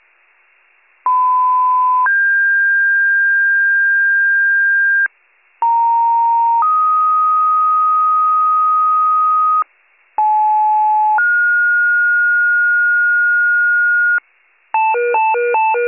Two-tone page, heard as a one-second tone, then a three-second tone, about 1010 then 1600 Hz, about 930 then 1250 Hz, about 860 then 1430 Hz, with noise. It is heard from 1.1 to 14.2 s.